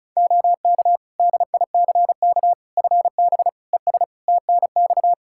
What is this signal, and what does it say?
Morse code, heard as a keyed tone, about 36 wpm, 700 Hz. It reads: OK DICK FB ES TNX
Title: Practice QSO - Exchange 2 - Info - 35wpm